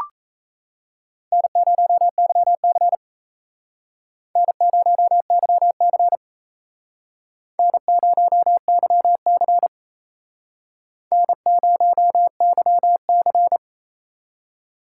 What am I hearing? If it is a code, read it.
Morse code, heard as a keyed tone, about 37 wpm, 700 Hz. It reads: N0YC N0YC N0YC N0YC